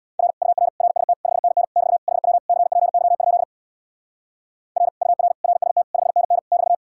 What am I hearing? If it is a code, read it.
Morse code, heard as a keyed tone, about 79 wpm, 700 Hz. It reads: R UR RST 5NN <BT> HW CPY? R UR RST 5NN <BT>